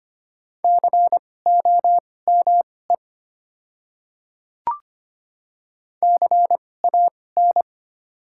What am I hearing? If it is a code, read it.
Morse code, heard as a keyed tone, about 25 wpm, spaced as for 18 wpm, 690 Hz. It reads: COME CAN
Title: 2x Character Spacing - Top 100 Words - 25wpm